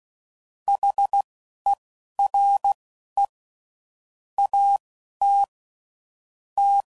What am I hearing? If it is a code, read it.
Morse code, heard as a keyed tone, about 16 wpm, spaced as for 11 wpm, 800 Hz. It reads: HERE AT T
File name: CW Code Groups @15WPM / QSOs / WD0BGP Spaced